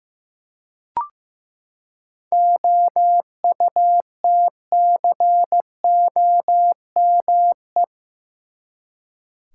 Morse code, heard as a keyed tone, about 15 wpm, 700 Hz. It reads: OUTCOME
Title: Seven Letter Words - 15wpm